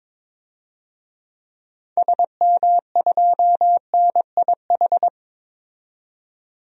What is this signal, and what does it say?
Morse code, heard as a keyed tone, about 22 wpm, 700 Hz. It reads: SM2NIH
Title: DXCC-100 Call Signs - 6 Characters - 22wpm